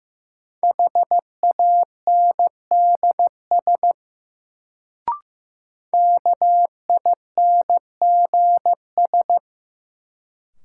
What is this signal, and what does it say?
Morse code, heard as a keyed tone, about 15 wpm, 700 Hz. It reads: HANDS KINGS
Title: Sentences from Top 1000 Words - Review Part 6 - New Word Drill - 15wpm